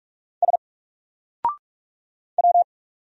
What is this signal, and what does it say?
Morse code, heard as a keyed tone, about 44 wpm, 700 Hz. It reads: S W
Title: Single Character (Letter:Number:Punctuation:Prosign) 45wpm